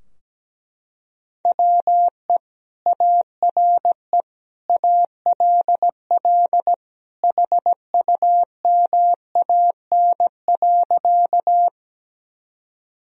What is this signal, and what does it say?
Morse code, heard as a keyed tone, about 17 wpm, 700 Hz. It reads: WE ARE ALL HUMAN.
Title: Sentences from Top 1000 Words - Review Part 3 - 17wpm